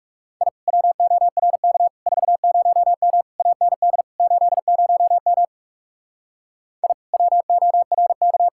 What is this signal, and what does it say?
Morse code, heard as a keyed tone, about 39 wpm, 700 Hz. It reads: I WORK 40M AND 80M I WORK